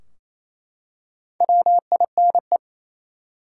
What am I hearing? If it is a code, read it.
Morse code, heard as a keyed tone, about 28 wpm, 700 Hz. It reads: WINE
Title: Four Letter Words - 28wpm